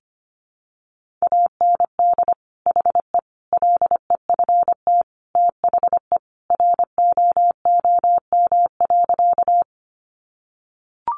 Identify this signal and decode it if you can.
Morse code, heard as a keyed tone, about 25 wpm, 700 Hz. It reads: AND HE LEFT THE ROOM.
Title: Sentences from Top 300 Words 25wpm